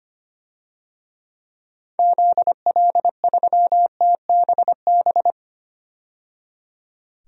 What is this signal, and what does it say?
Morse code, heard as a keyed tone, about 25 wpm, 700 Hz. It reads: ZL3TBB
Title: DXCC-100 Call Signs - 6 Characters - 25wpm